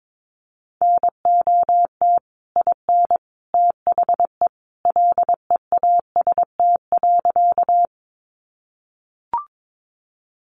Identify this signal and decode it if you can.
Morse code, heard as a keyed tone, about 22 wpm, 700 Hz. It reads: NOT IN THE LEAST.
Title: Sentences from Top 700 Words 22wpm